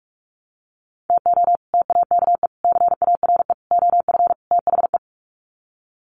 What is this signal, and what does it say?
Morse code, heard as a keyed tone, about 44 wpm, 700 Hz. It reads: TO TAKE CARE OF THE
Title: Sets of 5 Words Encore 45wpm